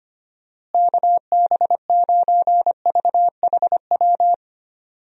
Morse code, heard as a keyed tone, about 25 wpm, 700 Hz. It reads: KB9VHW
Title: USA Call Signs - 6 Characters - Repeated 2x - 25wpm